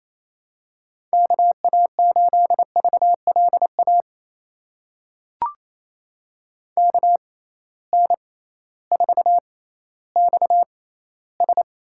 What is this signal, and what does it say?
Morse code, heard as a keyed tone, about 28 wpm, spaced as for 15 wpm, 700 Hz. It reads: KA8VLA KN4XS
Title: USA Call Signs - 6x Character Spacing - 28wpm